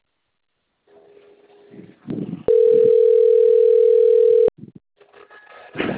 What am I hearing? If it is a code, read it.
Ringback tone (North America), 440 + 480 Hz.